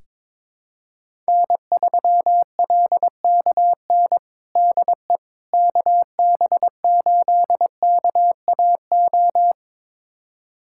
Morse code, heard as a keyed tone, about 22 wpm, 700 Hz. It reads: N3LKN DE KB8KAO
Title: USA Callsign DE Callsign 22wpm